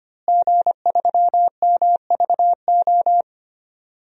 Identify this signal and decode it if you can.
Morse code, heard as a keyed tone, about 25 wpm, 700 Hz. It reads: G3MVO
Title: DXCC-100 Call Signs - 5 Characters - Repeated 2x - 25wpm